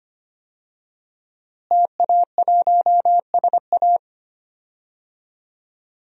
Morse code, heard as a keyed tone, about 25 wpm, 700 Hz. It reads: TA1SA
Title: DXCC-100 Call Signs - 5 Characters - Repeated 2x - 25wpm